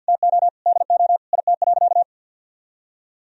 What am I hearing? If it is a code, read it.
Morse code, heard as a keyed tone, about 50 wpm, 700 Hz. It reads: TO DO IT.
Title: Sentences from Top 1000 Words - Review Part 3 - 50wpm